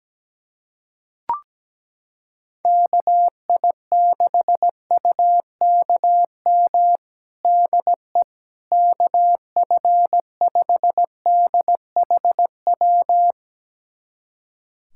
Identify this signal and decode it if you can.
Morse code, heard as a keyed tone, about 17 wpm, 700 Hz. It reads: KI6UKM DE KF5DHW
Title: USA Callsign DE Callsign - 6 Characters - 17wpm